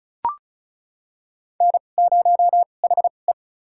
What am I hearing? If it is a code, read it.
Morse code, heard as a keyed tone, about 36 wpm, spaced as for 25 wpm, 700 Hz. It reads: N0HE
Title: USA Call Signs - 4 Characters - 2x Character Spacing - 35wpm